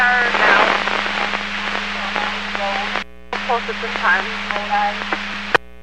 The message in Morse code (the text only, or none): none